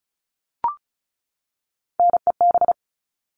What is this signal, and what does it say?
Morse code, heard as a keyed tone, about 34 wpm, 700 Hz. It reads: NEB
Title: All Cut Numbers - 3-Digit 35wpm